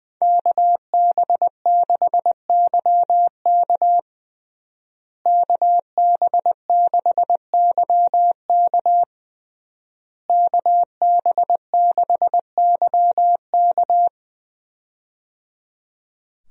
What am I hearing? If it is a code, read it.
Morse code, heard as a keyed tone, about 20 wpm, 700 Hz. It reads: KB6YK KB6YK KB6YK
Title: USA Call Signs - 5 Characters - Repeated 3x - 20wpm